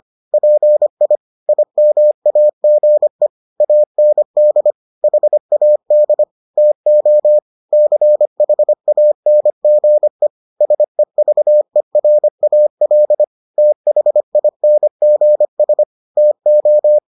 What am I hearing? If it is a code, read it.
Morse code, heard as a keyed tone, about 25 wpm, 600 Hz. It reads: PI IMAGE AND HAD TO CHANGE SEVERAL THINGS TO